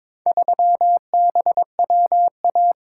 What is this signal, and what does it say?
Morse code, heard as a keyed tone, about 22 wpm, 700 Hz. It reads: 3BWA
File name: practice / 22wpm / Four Letter-Number Combinations - 22wpm